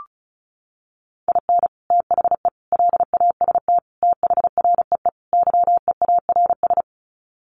Morse code, heard as a keyed tone, about 34 wpm, 700 Hz. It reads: IN THE LAST THREE YEARS
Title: Sets of 5 Words Encore2 35wpm